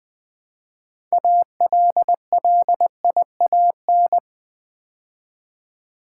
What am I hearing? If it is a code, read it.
Morse code, heard as a keyed tone, about 20 wpm, 700 Hz. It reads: ALLIAN
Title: Build A Word - 3 Letters At A Time - 8-Letter Words - 20wpm